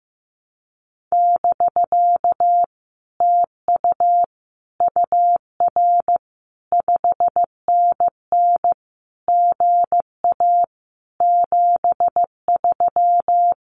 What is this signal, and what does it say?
Morse code, heard as a keyed tone, about 15 wpm, 700 Hz. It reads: <BK> TU UR 5NN GA 73
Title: Common QSO Phrases Encore 15wpm